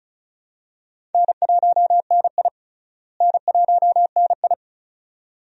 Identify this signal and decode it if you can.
Morse code, heard as a keyed tone, about 36 wpm, 700 Hz. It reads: N1NI N1NI
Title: USA Call Signs - 4 Characters - Repeated 2x - 35wpm